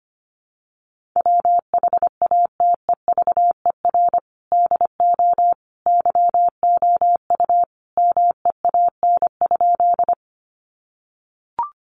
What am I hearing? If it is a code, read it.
Morse code, heard as a keyed tone, about 25 wpm, 700 Hz. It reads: WHATEVER DO YOU MEAN?
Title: Sentences from Top 1100 Words 25wpm